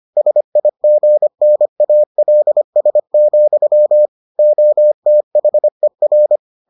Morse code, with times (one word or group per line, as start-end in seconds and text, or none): 0.2-4.1 SIGNALS,
4.4-6.4 OTHER